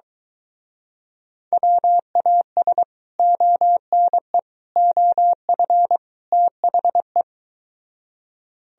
Morse code, heard as a keyed tone, about 22 wpm, 700 Hz. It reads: WAS ONE OF THE